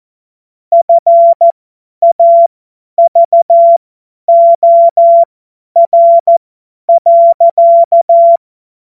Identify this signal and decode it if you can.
Morse code, heard as a keyed tone, about 14 wpm, spaced as for 10 wpm, 680 Hz. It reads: FAVOR.
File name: Morsekurs / Witze / Witz 14. Dezember 2021, 50 F70